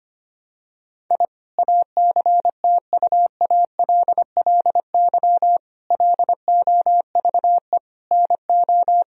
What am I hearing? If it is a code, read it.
Morse code, heard as a keyed tone, about 25 wpm, 700 Hz. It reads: I ACTUALLY LOVE NO
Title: Sentences from Top 1000 Words - Review Part 4 - 25wpm